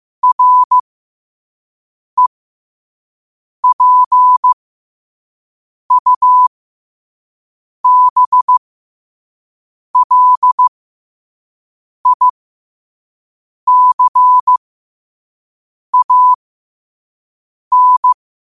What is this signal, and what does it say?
Morse code, heard as a keyed tone, about 15 wpm, spaced as for 5 wpm, 1000 Hz. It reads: REPUBLICAN
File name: qotdfiles / 5wpm